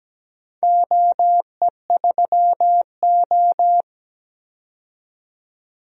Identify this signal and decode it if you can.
Morse code, heard as a keyed tone, about 17 wpm, 700 Hz. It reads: OE3O
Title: DXCC-100 Call Signs - 4 Characters - 17wpm